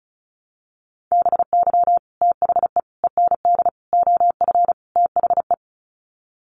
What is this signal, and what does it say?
Morse code, heard as a keyed tone, about 34 wpm, 700 Hz. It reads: BY THE END OF THE
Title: Sets of 5 Words 35wpm